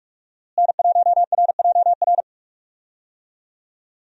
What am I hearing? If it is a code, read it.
Morse code, heard as a keyed tone, about 41 wpm, 700 Hz. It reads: N1RJR
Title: USA Call Signs - 5 Characters - Repeated 3x - 45wpm